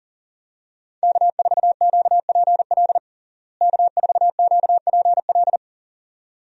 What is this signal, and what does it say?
Morse code, heard as a keyed tone, about 37 wpm, 700 Hz. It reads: K4QPL K4QPL
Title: USA Call Signs - 5 Characters - Repeated 2x - 40wpm